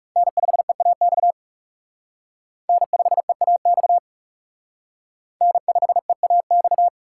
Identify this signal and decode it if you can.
Morse code, heard as a keyed tone, about 37 wpm, 700 Hz. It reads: N5EAX N5EAX N5EAX
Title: Speed Racing - USA 1x3 Call Signs 30wpm